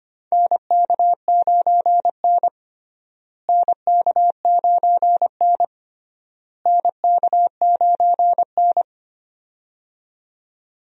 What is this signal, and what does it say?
Morse code, heard as a keyed tone, about 25 wpm, 700 Hz. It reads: NK9N NK9N NK9N
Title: USA Call Signs - 4 Characters - Repeated 3x - 25wpm